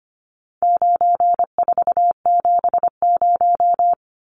Morse code, heard as a keyed tone, about 25 wpm, 700 Hz. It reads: 9470
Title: Numbers - 4-Digit 25wpm